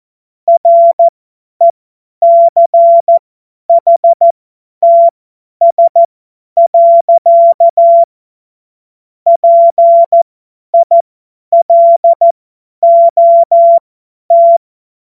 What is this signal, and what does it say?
Morse code, heard as a keyed tone, about 14 wpm, spaced as for 10 wpm, 680 Hz. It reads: RECHTS. PILOT
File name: Morsekurs / Witze / Witz 25. Juli 2023 50 F70